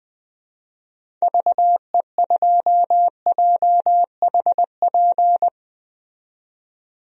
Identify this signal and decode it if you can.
Morse code, heard as a keyed tone, about 20 wpm, 700 Hz. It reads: VE2JHP
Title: DXCC-100 Call Signs - 6 Characters - 20wpm